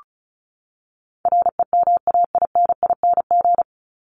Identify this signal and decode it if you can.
Morse code, heard as a keyed tone, about 34 wpm, 700 Hz. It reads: REMAINING
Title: ING Suffix - Top 100 Words with Base Words 35wpm